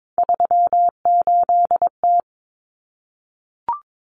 Morse code, heard as a keyed tone, about 22 wpm, 700 Hz. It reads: 38T
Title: Common Cut Numbers - 3-Digit 22wpm